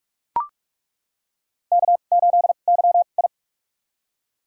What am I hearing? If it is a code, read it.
Morse code, heard as a keyed tone, about 42 wpm, spaced as for 31 wpm, 700 Hz. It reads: K8YI
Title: USA Call Signs - 4 Characters - 2x Character Spacing - 45wpm